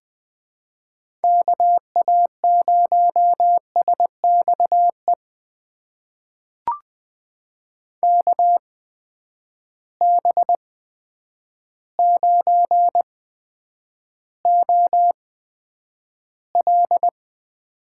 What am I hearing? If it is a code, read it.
Morse code, heard as a keyed tone, about 20 wpm, spaced as for 9 wpm, 700 Hz. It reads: KA0SXE KB9OL
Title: USA Call Signs - 6 Characters - 8x Character Spacing - 20wpm